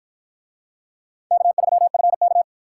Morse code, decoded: K3FK

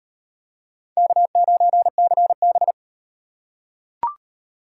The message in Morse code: K9CB

